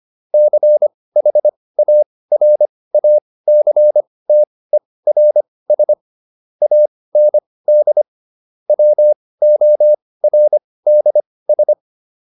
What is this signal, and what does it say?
Morse code, heard as a keyed tone, about 25 wpm, spaced as for 18 wpm, 600 Hz. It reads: CHARACTERS AND WORDS